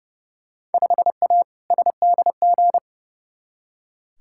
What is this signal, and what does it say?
Morse code, heard as a keyed tone, about 30 wpm, 700 Hz. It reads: 5A SDG
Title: Field Day Exchanges Encore - 30wpm